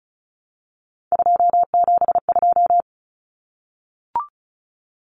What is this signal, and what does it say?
Morse code, heard as a keyed tone, about 34 wpm, 700 Hz. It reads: 272
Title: Numbers - 3-Digit 35wpm